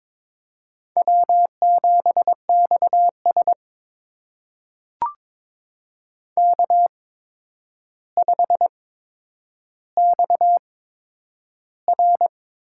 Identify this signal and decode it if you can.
Morse code, heard as a keyed tone, about 22 wpm, spaced as for 9 wpm, 700 Hz. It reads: W7XS K5XR
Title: USA Call Signs - 4 Characters - 8x Character Spacing - 22wpm